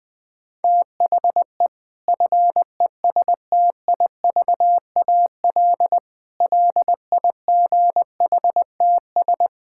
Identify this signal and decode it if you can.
Morse code, heard as a keyed tone, about 21 wpm, 700 Hz. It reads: THE FESTIVAL LIGHTS
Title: Difficult Sentences with Hints - 20wpm